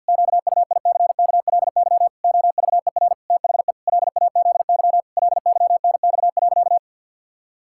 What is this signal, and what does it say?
Morse code, heard as a keyed tone, about 50 wpm, 700 Hz. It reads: QUICKLY OVER THE LAZY LYNX.
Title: Pangram Challenge v2 - 50wpm